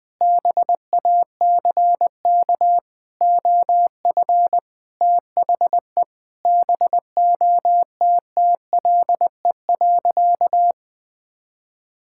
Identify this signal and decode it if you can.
Morse code, heard as a keyed tone, about 21 wpm, 700 Hz. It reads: BACK OF THE BOTTLE.
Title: Sentences with 4-Word Phrases - 20wpm